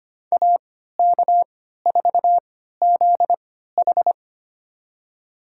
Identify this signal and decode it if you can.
Morse code, heard as a keyed tone, about 25 wpm, spaced as for 14 wpm, 700 Hz. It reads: AK4ZH
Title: USA Call Signs - 3x Character Spacing - 25wpm